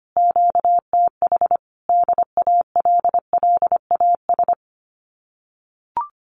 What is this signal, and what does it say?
Morse code, heard as a keyed tone, about 25 wpm, 700 Hz. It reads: QTH DALLAS
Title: Common QSO Phrases Encore 25wpm